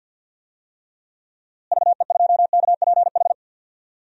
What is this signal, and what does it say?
Morse code, heard as a keyed tone, about 46 wpm, 700 Hz. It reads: VE2KPH
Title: DXCC-100 Call Signs - 6 Characters - Repeated 2x - 50wpm